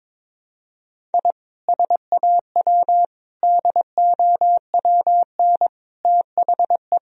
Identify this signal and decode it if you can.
Morse code, heard as a keyed tone, about 22 wpm, 700 Hz. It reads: I SAW DOWN THE
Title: Sentences from Top 1000 Words - Review Part 1 - 22wpm